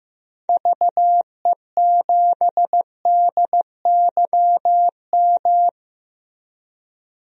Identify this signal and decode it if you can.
Morse code, heard as a keyed tone, about 15 wpm, 700 Hz. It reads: VE7DYM
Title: DXCC-100 Call Signs - 6 Characters - Repeated 2x - 15wpm